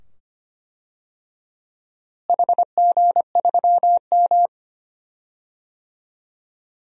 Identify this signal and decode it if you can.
Morse code, heard as a keyed tone, about 25 wpm, 700 Hz. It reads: HG3M
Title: DXCC-100 Call Signs - 4 Characters - Repeated 2x - 25wpm